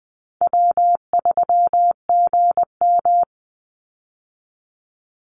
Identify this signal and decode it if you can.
Morse code, heard as a keyed tone, about 20 wpm, 700 Hz. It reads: W3GM
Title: USA Call Signs 20wpm